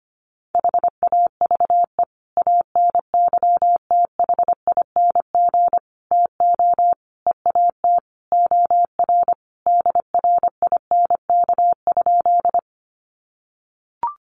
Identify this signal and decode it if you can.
Morse code, heard as a keyed tone, about 25 wpm, 700 Hz. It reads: HAVE ANYTHING TO EAT OR DRINK?
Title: Sentences from Top 1500 Words 25wpm